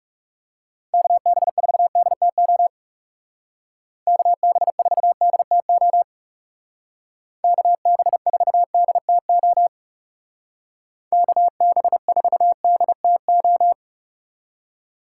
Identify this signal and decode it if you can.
Morse code, heard as a keyed tone, about 38 wpm, 700 Hz. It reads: KB4DTO KB4DTO KB4DTO KB4DTO